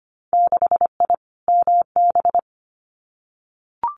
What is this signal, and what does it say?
Morse code, heard as a keyed tone, about 25 wpm, 690 Hz. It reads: 6I MB
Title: Winter Field Day Exchanges Encore 25wpm